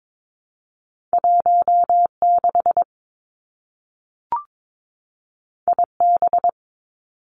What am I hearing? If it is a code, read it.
Morse code, heard as a keyed tone, about 22 wpm, 700 Hz. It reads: 1 6 I B
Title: Two Challenging Letter-Number Combinations 22wpm